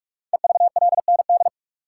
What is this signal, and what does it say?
Morse code, heard as a keyed tone, about 44 wpm, 700 Hz. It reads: EVLND
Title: Code Groups - 5 Letter:Numbers - 45wpm